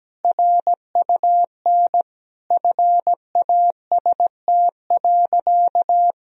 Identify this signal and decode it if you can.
Morse code, heard as a keyed tone, about 17 wpm, 700 Hz. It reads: RUN FAST.
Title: Sentences from Top 1000 Words - Review Part 2 - 17wpm